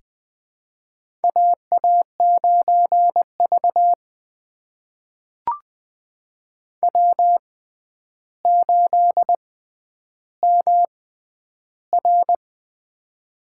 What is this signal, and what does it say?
Morse code, heard as a keyed tone, about 20 wpm, spaced as for 10 wpm, 700 Hz. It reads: AA9V W8MR
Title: USA Call Signs - 4 Characters - 6x Character Spacing - 20wpm